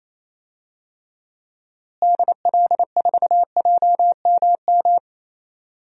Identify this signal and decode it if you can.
Morse code, heard as a keyed tone, about 28 wpm, 700 Hz. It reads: DL4JMM
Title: DXCC-100 Call Signs - 6 Characters - Repeated 2x - 28wpm